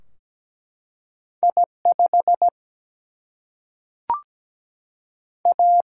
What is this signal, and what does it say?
Morse code, heard as a keyed tone, about 17 wpm, 700 Hz. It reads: I5 A
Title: Single-Letter Single-Number 17wpm